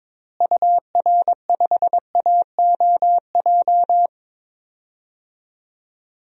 Morse code, heard as a keyed tone, about 21 wpm, 700 Hz. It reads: UR5AOJ
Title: DXCC-100 Call Signs - 6 Characters - Repeated 2x - 22wpm